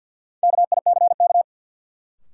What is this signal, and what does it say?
Morse code, heard as a keyed tone, about 53 wpm, 700 Hz. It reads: KICK
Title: Sentences from Top 1300 Words - New Word Drill Alt - 50wpm